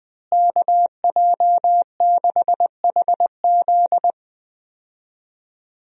Morse code, heard as a keyed tone, about 20 wpm, 700 Hz. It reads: KJ6HZ
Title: USA Call Signs - 5 Characters - Repeated 2x - 20wpm